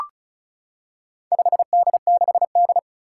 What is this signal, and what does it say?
Morse code, heard as a keyed tone, about 36 wpm, 700 Hz. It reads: 5D6D
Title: Four Challenging Letter-Number Combinations 35wpm